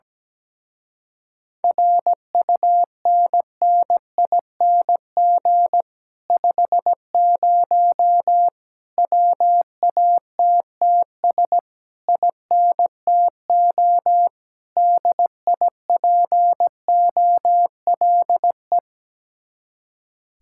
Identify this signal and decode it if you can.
Morse code, heard as a keyed tone, about 17 wpm, 700 Hz. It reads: RUNNING 50 WATTS INTO DIPOLE
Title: Speed-Racing Common QSO Phrases - 17wpm